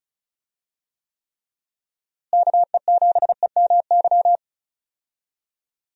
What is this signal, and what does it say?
Morse code, heard as a keyed tone, about 36 wpm, 700 Hz. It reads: KE7EMY
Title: USA Call Signs - 6 Characters - Repeated 3x - 35wpm